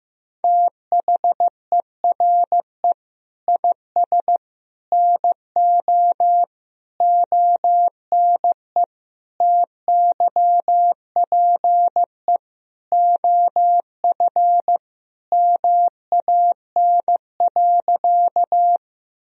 Morse code, heard as a keyed tone, about 16 wpm, 700 Hz. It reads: THERE IS NO ONE TYPE OF MAN.
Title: Sentences from Top 1000 Words - Review Part 3 - 15wpm